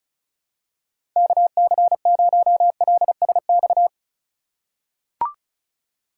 Morse code, heard as a keyed tone, about 34 wpm, 700 Hz. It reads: KC0LSX
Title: USA Call Signs - 2x Character Spacing - 35wpm